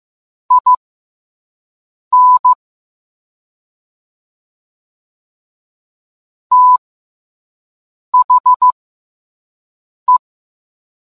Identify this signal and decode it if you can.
Morse code, heard as a keyed tone, about 15 wpm, spaced as for 5 wpm, 1000 Hz. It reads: IN THE